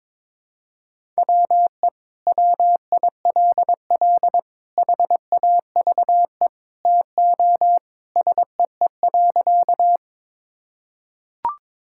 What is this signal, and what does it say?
Morse code, heard as a keyed tone, about 22 wpm, 700 Hz. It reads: WE WILL HAVE TO SEE.
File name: practice / 22wpm / Sentences with 3-Word Phrases - 22wpm